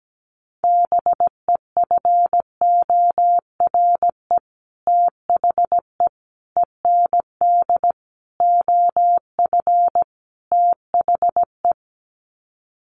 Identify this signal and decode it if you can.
Morse code, heard as a keyed tone, about 17 wpm, 690 Hz. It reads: BEFORE THE END OF THE
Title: Sets of 5 Words Encore 17wpm